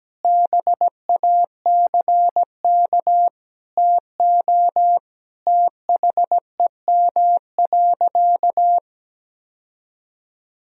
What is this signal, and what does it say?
Morse code, heard as a keyed tone, about 17 wpm, 700 Hz. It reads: BACK TO THEM.
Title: Sentences from Top 1000 Words - Review Part 3 - 17wpm